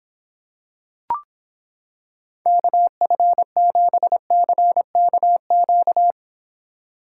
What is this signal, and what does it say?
Morse code, heard as a keyed tone, about 26 wpm, 700 Hz. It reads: KF7CKQ